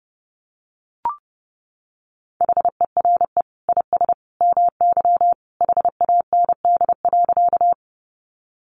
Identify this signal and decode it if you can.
Morse code, heard as a keyed tone, about 30 wpm, 700 Hz. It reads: HERE IS MY HAND.
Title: Sentences from Top 200 Words 30wpm